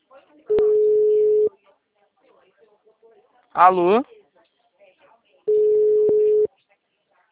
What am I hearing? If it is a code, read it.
Ringback tone (Europe, 425 Hz), with speech.
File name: out- -2009- - - .75396